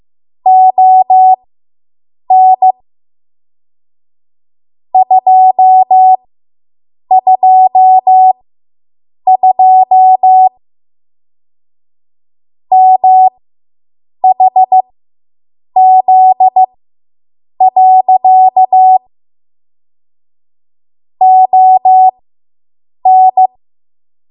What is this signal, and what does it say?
Morse code, heard as a keyed tone, about 15 wpm, spaced as for 7 wpm, 750 Hz. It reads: ON 222 MHZ. ON